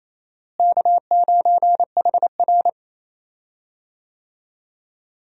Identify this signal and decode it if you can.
Morse code, heard as a keyed tone, about 28 wpm, 700 Hz. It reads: K9HR
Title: USA Call Signs - 4 Characters - Repeated 3x - 28wpm